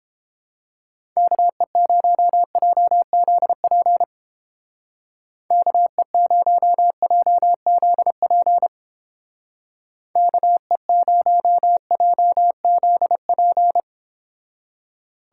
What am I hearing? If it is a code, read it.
Morse code, heard as a keyed tone, about 29 wpm, 700 Hz. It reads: KE0JZP KE0JZP KE0JZP